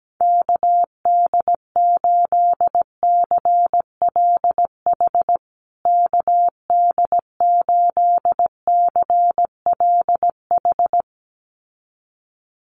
Morse code, heard as a keyed tone, about 17 wpm, 700 Hz. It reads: KD8CLH KD8CLH
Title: USA Call Signs Repeated 2x 17wpm